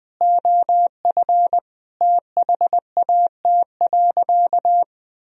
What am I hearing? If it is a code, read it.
Morse code, heard as a keyed tone, about 20 wpm, 700 Hz. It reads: OF THAT.
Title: Sentences from Top 1000 Words - Review Part 4 - 20wpm